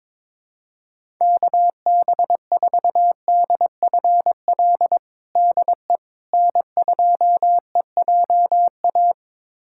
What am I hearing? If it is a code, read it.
Morse code, heard as a keyed tone, about 22 wpm, 700 Hz. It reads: KB4DFL DE N2EJA
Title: USA Callsign DE Callsign 22wpm